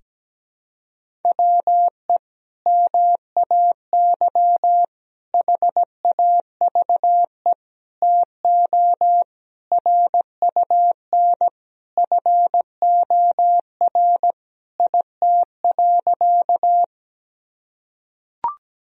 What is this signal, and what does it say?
Morse code, heard as a keyed tone, about 17 wpm, 700 Hz. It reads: WE MAY HAVE TO RUN FOR IT.
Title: Sentences from Top 1000 Words - Review Part 1 - 17wpm